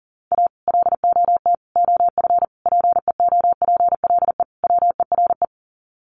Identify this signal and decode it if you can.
Morse code, heard as a keyed tone, about 40 wpm, 700 Hz. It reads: A LOT OF PEOPLE WERE
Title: Sets of 5 Words Encore2 40wpm